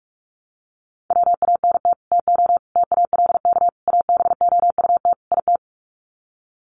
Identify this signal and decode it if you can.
Morse code, heard as a keyed tone, about 44 wpm, 700 Hz. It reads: WANT TO TALK ABOUT IT
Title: Sets of 5 Words Encore 45wpm